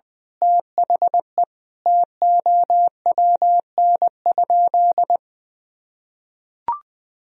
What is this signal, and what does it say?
Morse code, heard as a keyed tone, about 20 wpm, 700 Hz. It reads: THE TOWN?
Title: Sentences from Top 1000 Words - Review Part 2 - 20wpm